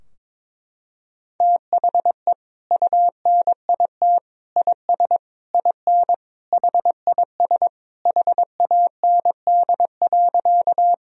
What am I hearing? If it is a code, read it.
Morse code, heard as a keyed tone, about 21 wpm, 700 Hz. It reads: THE UNIT IS IN HIS HAND.
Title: Sentences from Top 1000 Words - Review Part 2 - 22wpm